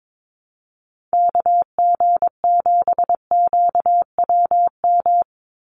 Morse code, heard as a keyed tone, about 22 wpm, 700 Hz. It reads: KG7QWM
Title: USA Call Signs 22wpm